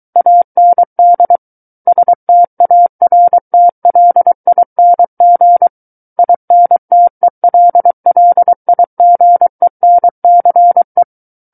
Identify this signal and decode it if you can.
Morse code, heard as a keyed tone, about 23 wpm, 700 Hz. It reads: AND STARTLING INTELLIGENCE
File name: Book One / 14-Chapter 14 - 23 WPM